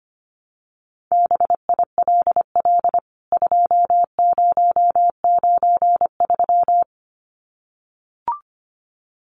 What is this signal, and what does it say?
Morse code, heard as a keyed tone, about 25 wpm, 700 Hz. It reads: BILL 2093
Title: CWT Exchange 25wpm